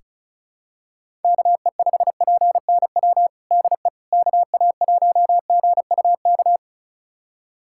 Morse code, heard as a keyed tone, about 36 wpm, 700 Hz. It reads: KE5PNW DE KA1GUK
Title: USA Callsign DE Callsign - 6 Characters - 35wpm